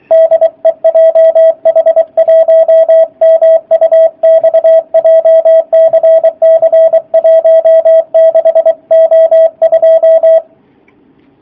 Morse code, heard as a keyed tone, about 24 wpm, 650 Hz, with radio noise. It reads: DE JH1MUX JCC 16O2
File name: jh8cbh / 2019cw / sp27- w